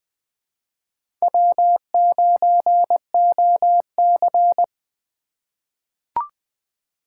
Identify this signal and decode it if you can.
Morse code, heard as a keyed tone, about 21 wpm, 700 Hz. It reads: W9OC
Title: USA Call Signs - 4 Characters - 2x Character Spacing - 20wpm